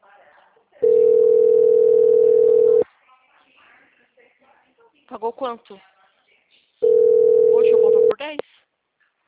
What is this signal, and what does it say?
Ringback tone (North America), 440 + 480 Hz, with speech.
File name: out- -2004- - - .49661